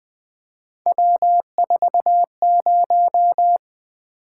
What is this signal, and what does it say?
Morse code, heard as a keyed tone, about 20 wpm, 700 Hz. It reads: W40
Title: Three Letter-Number Combinations - 20wpm